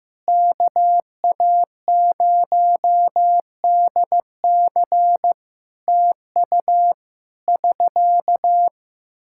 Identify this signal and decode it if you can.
Morse code, heard as a keyed tone, about 15 wpm, 700 Hz. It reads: KA0DC TU <SK>
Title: Practice QSO - Exchange 4 - Ending - 15wpm